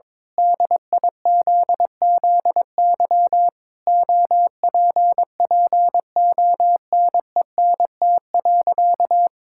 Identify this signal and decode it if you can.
Morse code, heard as a keyed tone, about 22 wpm, 700 Hz. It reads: DIZZY OPPONENT.
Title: Pangram Challenge v2 - 22wpm